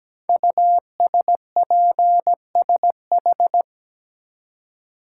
Morse code, heard as a keyed tone, about 17 wpm, 700 Hz. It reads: USPSH
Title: Code Groups - 5 Letters - 17wpm